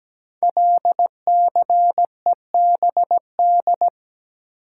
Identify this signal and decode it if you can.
Morse code, heard as a keyed tone, about 17 wpm, 700 Hz. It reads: LCEBD
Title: Code Groups - 5 Letters - 17wpm